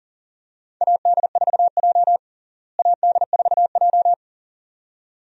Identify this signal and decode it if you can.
Morse code, heard as a keyed tone, about 40 wpm, 700 Hz. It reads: AD4J AD4J
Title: USA Call Signs - 4 Characters - Repeated 2x - 40wpm